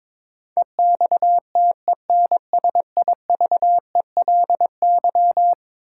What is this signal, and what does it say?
Morse code, heard as a keyed tone, about 22 wpm, 700 Hz. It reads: EXTENSIVELY
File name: practice / 22wpm / Eleven Letter Words - 22wpm